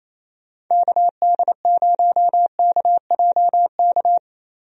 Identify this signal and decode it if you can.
Morse code, heard as a keyed tone, about 28 wpm, 700 Hz. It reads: KD0KJK
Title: USA Call Signs - 6 Characters - Repeated 3x - 28wpm